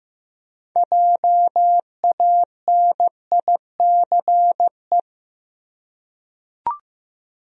Morse code, heard as a keyed tone, about 15 wpm, 700 Hz. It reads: JANICE
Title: USA Common Names - 15wpm